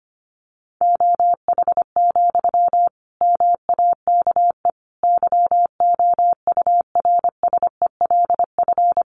Morse code, heard as a keyed tone, about 25 wpm, 700 Hz. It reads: OH, MAKE YOURSELF
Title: Sentences from Top 700 Words 25wpm